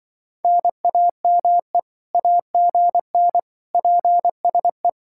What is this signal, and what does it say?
Morse code, heard as a keyed tone, about 25 wpm, 700 Hz. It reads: NAME AGN PSE
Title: Speed-Racing Common QSO Phrases Encore - 20wpm